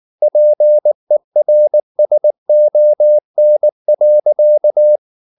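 Morse code, heard as a keyed tone, about 19 wpm, 600 Hz. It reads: PERSON.
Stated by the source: Morse code at 19 wpm from CW med Gnister, reading Ken Follett's novel Never in English.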